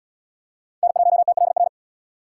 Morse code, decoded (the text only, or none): N1TLR